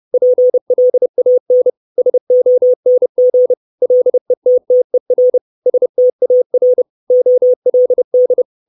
Morse code, heard as a keyed tone, about 30 wpm, 500 Hz. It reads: PLAN SONG LETTER STAR OLD
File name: 632 Words 30WPM